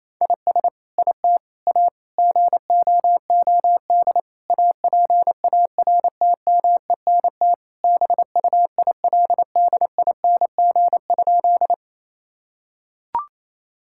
Morse code, heard as a keyed tone, about 28 wpm, 700 Hz. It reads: IS IT A GOOD APARTMENT BUILDING?
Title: Sentences from Top 1000 Words - Review Part 6 - 28wpm